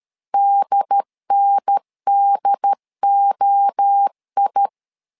Morse code, heard as a keyed tone, about 13 wpm, 800 Hz. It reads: DNDOI
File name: LEZIONI / 3 MWRO / 3 MWRO 12wpm 1730